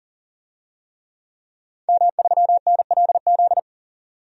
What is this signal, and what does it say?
Morse code, heard as a keyed tone, about 40 wpm, 700 Hz. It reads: M3NLZ
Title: DXCC-100 Call Signs - 5 Characters - 40wpm